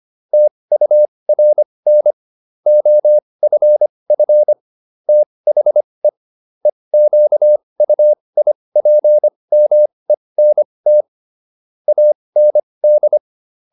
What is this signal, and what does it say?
Morse code, heard as a keyed tone, about 25 wpm, spaced as for 20 wpm, 600 Hz. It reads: TURN OFF THE EQUIPMENT AND